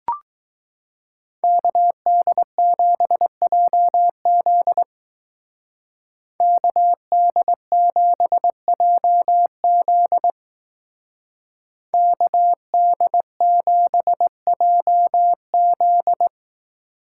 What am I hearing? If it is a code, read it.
Morse code, heard as a keyed tone, about 19 wpm, 700 Hz. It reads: KD7JZ KD7JZ KD7JZ
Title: Speed Racing - USA Call Signs 15wpm